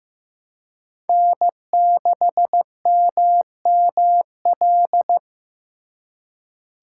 Morse code, heard as a keyed tone, about 15 wpm, 700 Hz. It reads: N6MML
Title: USA Call Signs - 5 Characters - 3x Character Spacing - 15wpm